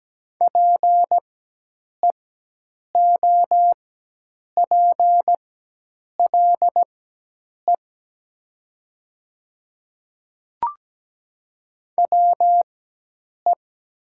Morse code, heard as a keyed tone, about 17 wpm, spaced as for 8 wpm, 700 Hz. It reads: PEOPLE WE
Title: 4x Character Spacing - Top 100 Words - 17wpm